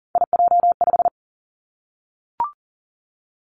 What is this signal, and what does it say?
Morse code, heard as a keyed tone, about 40 wpm, 700 Hz. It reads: IJ5
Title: Three Challenging Letter-Number Combinations 40wpm